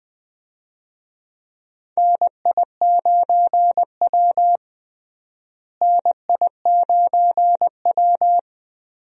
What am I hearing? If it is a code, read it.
Morse code, heard as a keyed tone, about 21 wpm, 700 Hz. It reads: NI9W NI9W
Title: USA Call Signs - 4 Characters - Repeated 3x - 20wpm